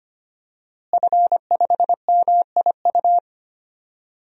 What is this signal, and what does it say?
Morse code, heard as a keyed tone, about 25 wpm, 700 Hz. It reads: F5MIU
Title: DXCC-100 Call Signs - 5 Characters - Repeated 2x - 25wpm